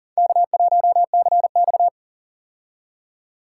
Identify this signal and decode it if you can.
Morse code, heard as a keyed tone, about 40 wpm, 700 Hz. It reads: K1CX
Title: K1USN SST QSO Practice - 40wpm